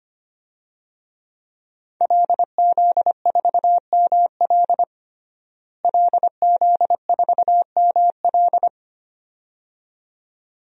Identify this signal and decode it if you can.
Morse code, heard as a keyed tone, about 25 wpm, 700 Hz. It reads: LZ4ML LZ4ML
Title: DXCC-100 Call Signs - 5 Characters - Repeated 2x - 25wpm